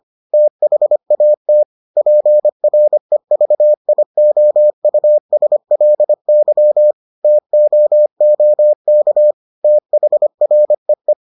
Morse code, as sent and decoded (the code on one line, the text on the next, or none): - .... .- - / .--. .-. ...- .. --- ..- ... .-.. -.-- / - --- --- -.- / - .... .-. .
THAT PREVIOUSLY TOOK THREE